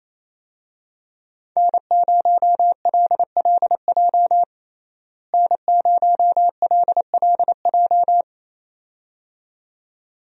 Morse code, heard as a keyed tone, about 28 wpm, 700 Hz. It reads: N0LLJ N0LLJ
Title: USA Call Signs - 5 Characters - Repeated 2x - 28wpm